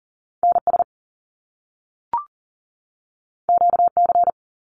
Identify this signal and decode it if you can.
Morse code, heard as a keyed tone, about 39 wpm, 700 Hz. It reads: NS QC
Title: Canadian Provinces and Territories 40wpm